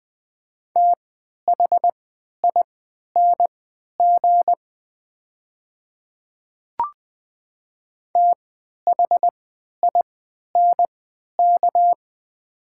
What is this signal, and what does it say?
Morse code, heard as a keyed tone, about 20 wpm, spaced as for 11 wpm, 700 Hz. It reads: THING THINK
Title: 3x Character Spacing - Top 500 Words - 20wpm